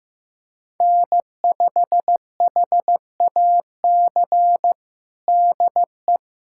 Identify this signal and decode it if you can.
Morse code, heard as a keyed tone, about 16 wpm, 700 Hz. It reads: N5HAC DE
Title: USA Callsign DE Callsign - 5 Characters - 15wpm